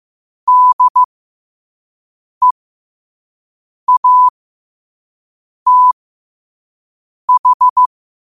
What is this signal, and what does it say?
Morse code, heard as a keyed tone, about 15 wpm, spaced as for 5 wpm, 1000 Hz. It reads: DEATH
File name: qotdfiles / 5wpm